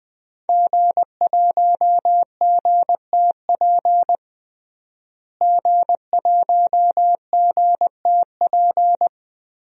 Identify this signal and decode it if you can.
Morse code, heard as a keyed tone, about 20 wpm, 700 Hz. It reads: G1GTP G1GTP
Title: DXCC-100 Call Signs - 5 Characters - Repeated 3x - 20wpm